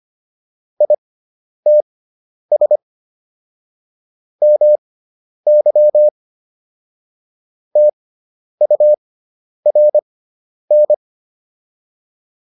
Morse code, heard as a keyed tone, about 25 wpm, spaced as for 10 wpm, 600 Hz. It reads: ITS MY TURN